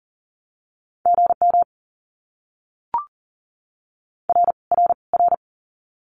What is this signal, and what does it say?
Morse code, heard as a keyed tone, about 38 wpm, 700 Hz. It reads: GM R R R
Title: Abbreviations Repeated 3x - 40wpm